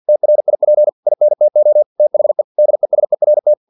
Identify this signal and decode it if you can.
Morse code, heard as a keyed tone, about 50 wpm, 600 Hz. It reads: TRIP INTO THE DESERT
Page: Never 49wpm